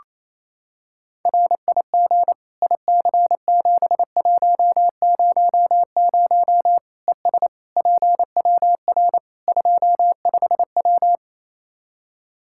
Morse code, heard as a keyed tone, about 28 wpm, 700 Hz. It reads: RIG IC7100 ES PWR 25W
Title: K1USN SST QSO Practice 28wpm